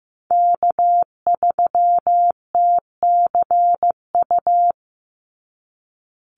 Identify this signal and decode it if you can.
Morse code, heard as a keyed tone, about 15 wpm, 700 Hz. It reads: K3TCU
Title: NA SOTA Chaser Callsigns 15wpm